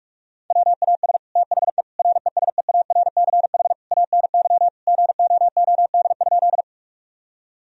Morse code, heard as a keyed tone, about 46 wpm, 700 Hz. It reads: WAS THE RESEARCH ANY GOOD?
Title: Sentences from Top 1000 Words - Review Part 4 - 45wpm